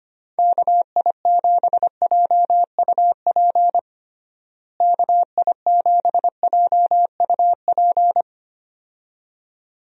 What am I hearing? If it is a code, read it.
Morse code, heard as a keyed tone, about 25 wpm, 700 Hz. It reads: KI7JUP KI7JUP
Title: USA Call Signs - 6 Characters - Repeated 2x - 25wpm